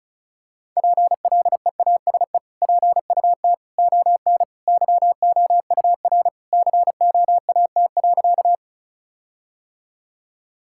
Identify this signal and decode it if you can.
Morse code, heard as a keyed tone, about 36 wpm, 700 Hz. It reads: PLEASE PUT ON YOUR COAT.
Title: Sentences from Top 1600 Words 35wpm